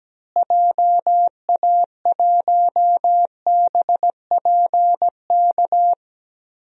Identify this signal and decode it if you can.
Morse code, heard as a keyed tone, about 17 wpm, 700 Hz. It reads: JA1BPK
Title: DXCC-100 Call Signs - 6 Characters - Repeated 3x - 17wpm